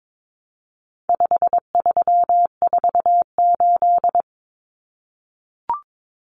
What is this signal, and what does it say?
Morse code, heard as a keyed tone, about 22 wpm, 700 Hz. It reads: 5348
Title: Common Cut Numbers - 4-Digit 22wpm